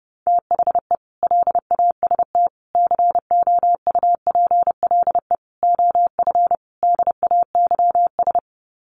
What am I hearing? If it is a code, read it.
Morse code, heard as a keyed tone, about 30 wpm, 700 Hz. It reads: THE LAST COUPLE OF DAYS
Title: Sets of 5 Words Encore 30wpm